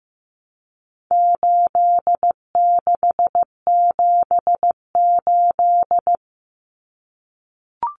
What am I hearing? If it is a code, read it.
Morse code, heard as a keyed tone, about 15 wpm, 700 Hz. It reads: 8678
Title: Numbers - 4-Digit 15wpm